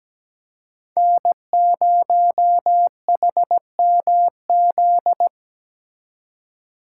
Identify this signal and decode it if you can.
Morse code, heard as a keyed tone, about 20 wpm, 700 Hz. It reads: N0HMZ
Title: Speed Racing - USA 1x3 Call Signs 17wpm